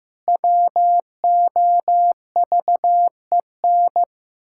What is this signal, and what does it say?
Morse code, heard as a keyed tone, about 15 wpm, 700 Hz. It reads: WOVEN
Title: Pangram Challenge v2 - 15wpm